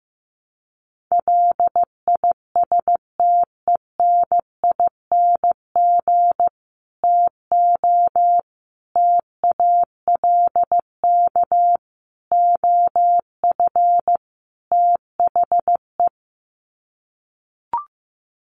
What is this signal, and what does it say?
Morse code, heard as a keyed tone, about 15 wpm, 700 Hz. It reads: LISTENING TO TALK OF THE
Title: Sets of 5 Words 15wpm